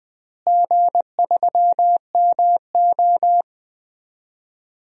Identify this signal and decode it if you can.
Morse code, heard as a keyed tone, about 20 wpm, 700 Hz. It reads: G3MO
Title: DXCC-100 Call Signs - 4 Characters - Repeated 3x - 20wpm